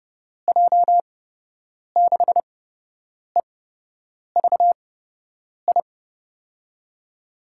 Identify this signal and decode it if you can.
Morse code, heard as a keyed tone, about 31 wpm, spaced as for 8 wpm, 700 Hz. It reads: J6EVI
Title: USA Call Signs - 6 Characters - 8x Character Spacing - 30wpm